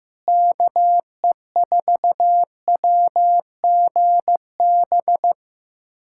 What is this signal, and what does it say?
Morse code, heard as a keyed tone, about 16 wpm, 700 Hz. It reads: KE4WGB
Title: USA Callsign DE Callsign 15wpm